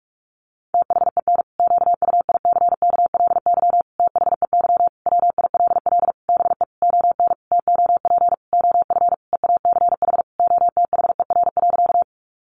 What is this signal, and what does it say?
Morse code, heard as a keyed tone, about 44 wpm, 700 Hz. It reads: THEN QUICKLY THEY WILL BE ON TOP OF EACH OTHER.
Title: Sentences from Top 700 Words 45wpm